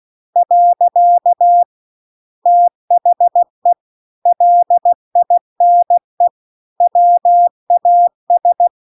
Morse code, heard as a keyed tone, about 16 wpm, 700 Hz. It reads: . THE LINE WAS